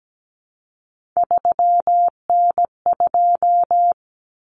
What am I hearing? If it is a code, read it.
Morse code, heard as a keyed tone, about 17 wpm, 700 Hz. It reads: 3N2
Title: Common Cut Numbers - 3-Digit 17wpm